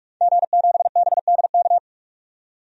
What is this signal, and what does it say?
Morse code, heard as a keyed tone, about 46 wpm, 700 Hz. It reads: G7BDK
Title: DXCC-100 Call Signs - 5 Characters - 45wpm